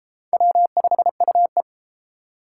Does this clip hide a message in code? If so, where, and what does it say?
Yes: Morse code from 0.3 to 1.6 s, heard as a keyed tone, about 30 wpm, 700 Hz. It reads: W5UE